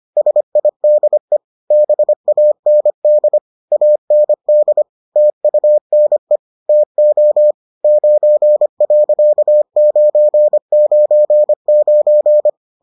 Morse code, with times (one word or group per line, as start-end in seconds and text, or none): 0.2-1.4 SIDE
1.7-3.4 BAND
3.7-4.8 AND
5.2-6.4 TUNE
6.7-7.5 TO
7.8-12.5 9.999